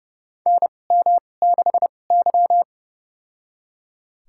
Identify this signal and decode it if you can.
Morse code, heard as a keyed tone, about 30 wpm, spaced as for 22 wpm, 700 Hz. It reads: NM6Y
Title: USA Call Signs - 4 Characters - 2x Character Spacing - 30wpm